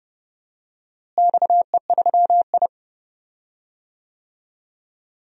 Morse code, heard as a keyed tone, about 31 wpm, 700 Hz. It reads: XE3I
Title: DXCC-100 Call Signs - 4 Characters - Repeated 2x - 30wpm